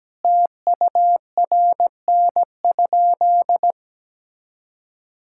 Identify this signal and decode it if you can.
Morse code, heard as a keyed tone, about 17 wpm, 700 Hz. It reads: TURN?
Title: Sentences from Top 1600 Words 17wpm